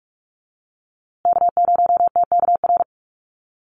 Morse code, heard as a keyed tone, about 44 wpm, 700 Hz. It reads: K0TKR
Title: USA Call Signs Repeated 2x 45wpm